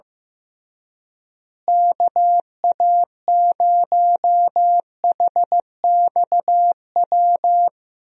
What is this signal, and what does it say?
Morse code, heard as a keyed tone, about 16 wpm, 700 Hz. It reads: KA0HXW
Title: USA Call Signs - 6 Characters - Repeated 2x - 15wpm